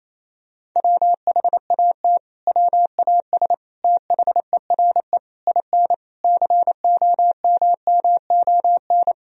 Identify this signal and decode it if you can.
Morse code, heard as a keyed tone, about 28 wpm, 690 Hz. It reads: WHAT WAS THERE IN COMMON
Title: Sentences from Top 1000 Words - Review Part 3 - 28wpm